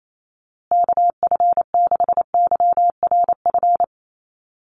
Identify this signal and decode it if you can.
Morse code, heard as a keyed tone, about 28 wpm, 700 Hz. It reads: KF6YRF
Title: USA Call Signs Repeated 2x 28wpm